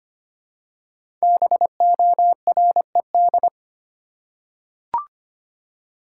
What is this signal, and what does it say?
Morse code, heard as a keyed tone, about 25 wpm, 700 Hz. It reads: BORED
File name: 5 Letter Words - First 3 Letters Given - 25wpm